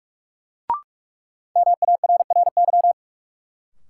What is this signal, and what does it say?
Morse code, heard as a keyed tone, about 46 wpm, 700 Hz. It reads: MARRY
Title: Sentences from Top 1200 Words - New Word Drill - 45wpm